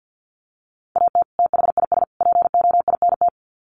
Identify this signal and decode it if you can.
Morse code, heard as a keyed tone, about 48 wpm, 700 Hz. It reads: AT THIS POINT
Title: Sets of 3 Words 50wpm